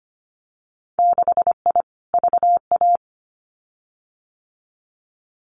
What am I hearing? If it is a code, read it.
Morse code, heard as a keyed tone, about 25 wpm, 700 Hz. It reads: 6I VA
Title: Winter Field Day Exchanges Encore 25wpm